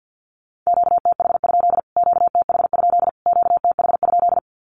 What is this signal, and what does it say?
Morse code, heard as a keyed tone, about 48 wpm, 700 Hz. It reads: QTH? QTH? QTH?
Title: Q-Codes Repeated 3x 50wpm